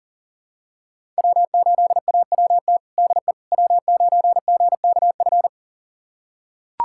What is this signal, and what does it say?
Morse code, heard as a keyed tone, about 40 wpm, 700 Hz. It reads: W8AWT DE W9GKF
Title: USA Callsign DE Callsign - 5 Characters - 40wpm